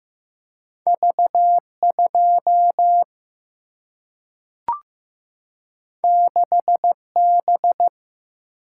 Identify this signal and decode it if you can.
Morse code, heard as a keyed tone, about 15 wpm, 700 Hz. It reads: V 2 6 B
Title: Single-Letter Single-Number 15wpm